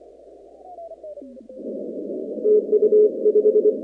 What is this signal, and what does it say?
Morse code, heard as a keyed tone, about 25 wpm, 420 Hz. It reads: TU 5
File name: contests / 2019-cw-wpx-kt5j-k5pi- -2330 0-19